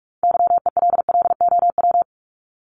Morse code, heard as a keyed tone, about 46 wpm, 700 Hz. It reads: YELLOW
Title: Top 1751-2000 Words 45wpm